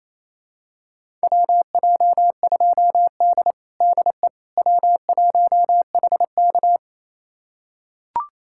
Morse code, heard as a keyed tone, about 28 wpm, 700 Hz. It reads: WJ2D DE W1HK
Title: USA Callsign DE Callsign - 4 Characters - 28wpm